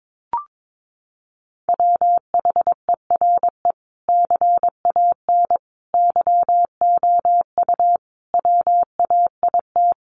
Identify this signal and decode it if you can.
Morse code, heard as a keyed tone, about 22 wpm, 700 Hz. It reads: WHERE CAN YOU WAIT
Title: Sentences from Top 500 Words 22wpm